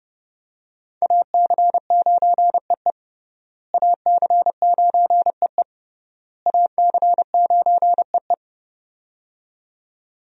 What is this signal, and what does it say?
Morse code, heard as a keyed tone, about 30 wpm, 700 Hz. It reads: AC9EE AC9EE AC9EE
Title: USA Call Signs - 5 Characters - Repeated 3x - 30wpm